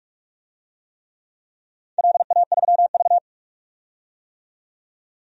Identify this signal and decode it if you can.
Morse code, heard as a keyed tone, about 41 wpm, 700 Hz. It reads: LA3V
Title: DXCC-100 Call Signs - 4 Characters - 45wpm